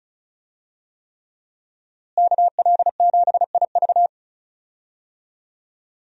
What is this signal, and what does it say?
Morse code, heard as a keyed tone, about 36 wpm, 700 Hz. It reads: KL7IV
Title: USA Call Signs - 5 Characters - Repeated 3x - 35wpm